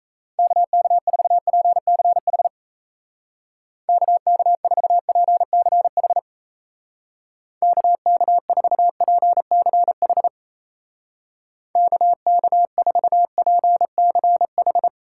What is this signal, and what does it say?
Morse code, heard as a keyed tone, about 36 wpm, 700 Hz. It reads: KK4PCH KK4PCH KK4PCH KK4PCH